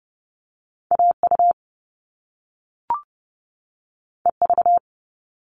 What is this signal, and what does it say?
Morse code, heard as a keyed tone, about 30 wpm, 700 Hz. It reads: AU EV
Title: Top Two Letter Combinations in Words 30wpm